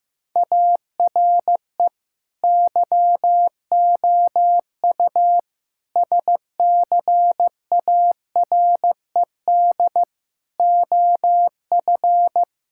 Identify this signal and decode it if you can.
Morse code, heard as a keyed tone, about 15 wpm, 700 Hz. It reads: ARE YOU SCARED OF
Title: Sentences from Top 1000 Words - Review Part 5 - 15wpm